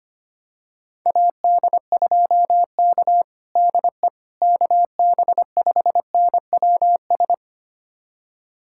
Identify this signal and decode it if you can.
Morse code, heard as a keyed tone, about 25 wpm, 700 Hz. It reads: AD2K DE KB5NWS
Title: USA Callsign DE Callsign 25wpm